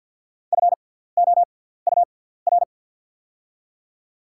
Morse code, heard as a keyed tone, about 48 wpm, spaced as for 17 wpm, 700 Hz. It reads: FOUR
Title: 6x Character Spacing - Top 300 Words - 50wpm